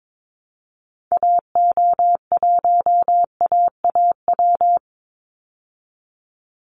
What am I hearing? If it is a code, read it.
Morse code, heard as a keyed tone, about 22 wpm, 700 Hz. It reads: AO1AAW
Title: Call Signs Repeated 3x - 22wpm